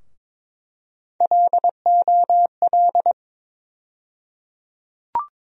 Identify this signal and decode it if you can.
Morse code, heard as a keyed tone, about 22 wpm, 700 Hz. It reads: LOL
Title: USA Call Sign Postfixes - 3 Letters 22wpm